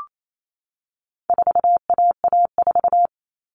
Morse code, heard as a keyed tone, about 28 wpm, 700 Hz. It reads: 4AA4
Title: All Cut Numbers - 4-Digit 28wpm